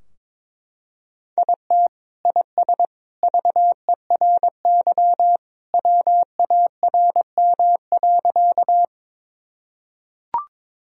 Morse code, heard as a keyed tone, about 21 wpm, 700 Hz. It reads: IT IS VERY WARM.
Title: Sentences from Top 1000 Words - Review Part 2 - 22wpm